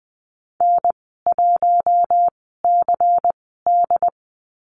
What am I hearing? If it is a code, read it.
Morse code, heard as a keyed tone, about 20 wpm, spaced as for 15 wpm, 700 Hz. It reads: N1CD
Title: 128 Lesson 21g - Number 6 - Call Signs 20wpm